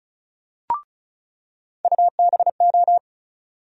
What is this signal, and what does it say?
Morse code, heard as a keyed tone, about 36 wpm, 700 Hz. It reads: UBO
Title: USA Call Sign Postfixes - 3 Letters 35wpm